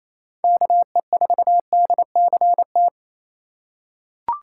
Morse code, heard as a keyed tone, about 27 wpm, 700 Hz. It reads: KE4DCT
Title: USA Call Signs - 6 Characters - 8x Character Spacing - 28wpm